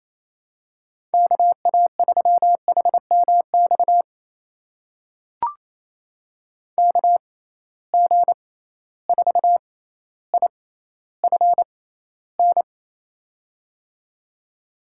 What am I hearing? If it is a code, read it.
Morse code, heard as a keyed tone, about 28 wpm, spaced as for 14 wpm, 700 Hz. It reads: KA3HMX KG4IFN
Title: USA Call Signs - 6 Characters - 6x Character Spacing - 28wpm